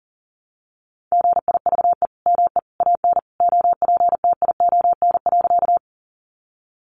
Morse code, heard as a keyed tone, about 40 wpm, 700 Hz. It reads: GIVE ME AN OPTION.
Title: Sentences from Top 1300 Words 40wpm